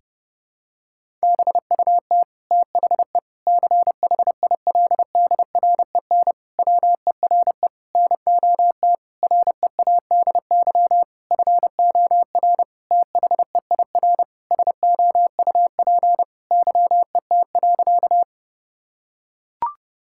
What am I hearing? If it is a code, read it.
Morse code, heard as a keyed tone, about 30 wpm, 700 Hz. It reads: BUT THE CHILDREN WERE NOT READY FOR THEIR SOUP YET.
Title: Sentences from Top 1000 Words - Review Part 6 - 30wpm